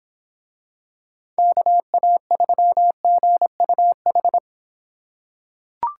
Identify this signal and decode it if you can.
Morse code, heard as a keyed tone, about 26 wpm, 700 Hz. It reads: KA3GUH